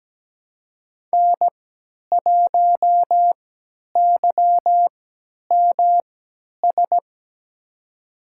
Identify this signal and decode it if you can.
Morse code, heard as a keyed tone, about 17 wpm, spaced as for 10 wpm, 700 Hz. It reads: N1YMS
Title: USA Call Signs - 5 Characters - 3x Character Spacing - 17wpm